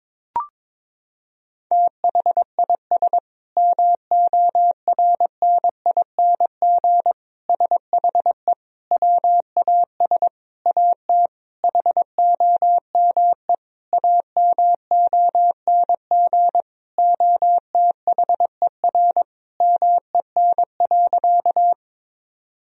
Morse code, heard as a keyed tone, about 22 wpm, 700 Hz. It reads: THIS MORNING SHE WAS AT HOME AMONG OTHER MEN.
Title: Sentences from Top 1000 Words - Review Part 3 - 22wpm